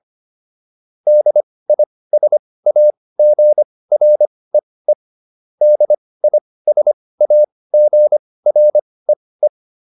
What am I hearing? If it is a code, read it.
Morse code, heard as a keyed tone, about 25 wpm, spaced as for 18 wpm, 600 Hz. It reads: DISAGREE DISAGREE
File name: DIS8 18